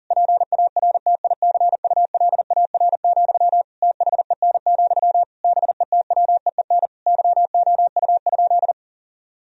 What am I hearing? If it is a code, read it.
Morse code, heard as a keyed tone, about 40 wpm, 700 Hz. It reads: PARTICULAR, THEN, BETWEEN YOU?
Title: Sentences from Top 1000 Words - Review Part 4 - 40wpm